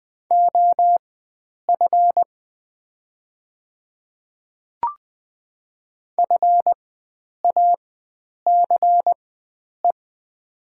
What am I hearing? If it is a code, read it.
Morse code, heard as a keyed tone, about 20 wpm, spaced as for 9 wpm, 700 Hz. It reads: OF FACE
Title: 4x Character Spacing - Top 300 Words - 20wpm